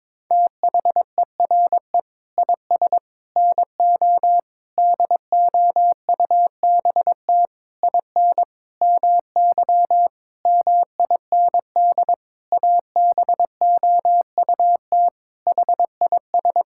Morse code, heard as a keyed tone, about 22 wpm, 700 Hz. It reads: THERE IS NO DOUBT IN MY MIND ABOUT HIS
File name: Sentences with 7-Word Phrases - 22wpm